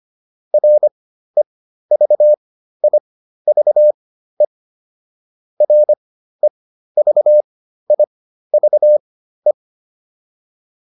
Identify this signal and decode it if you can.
Morse code, heard as a keyed tone, about 25 wpm, spaced as for 13 wpm, 600 Hz. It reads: REVIVE REVIVE